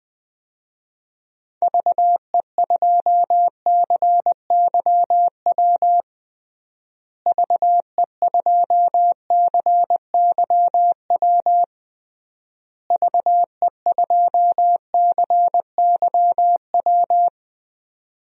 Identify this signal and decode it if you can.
Morse code, heard as a keyed tone, about 20 wpm, 700 Hz. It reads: VE2CYW VE2CYW VE2CYW
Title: DXCC-100 Call Signs - 6 Characters - Repeated 3x - 20wpm